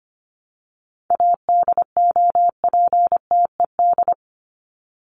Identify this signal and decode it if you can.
Morse code, heard as a keyed tone, about 25 wpm, 700 Hz. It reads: ADOPTED
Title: ED Suffix - Top 100 Words with Base Words 25wpm